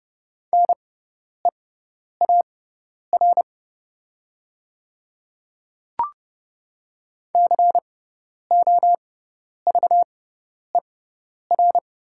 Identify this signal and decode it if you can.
Morse code, heard as a keyed tone, about 30 wpm, spaced as for 10 wpm, 700 Hz. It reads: NEAR COVER
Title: 6x Character Spacing - Top 300 Words - 30wpm